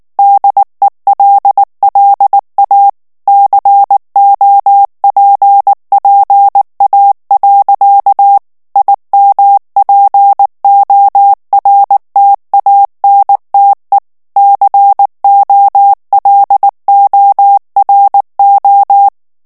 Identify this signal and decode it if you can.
Morse code, heard as a keyed tone, about 19 wpm, 800 Hz. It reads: DELLA COPPA. IMPORTANTE COLORO
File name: morse mp3 / ITALIEN texte clair 2 vitesse 18 mots minute